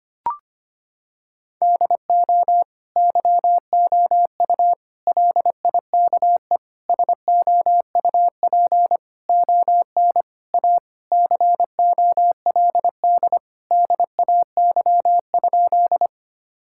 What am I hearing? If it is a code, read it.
Morse code, heard as a keyed tone, about 25 wpm, 700 Hz. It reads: DO YOU LIKE SOUP ON A COLD DAY?
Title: Sentences from Top 1000 Words - Review Part 6 - 25wpm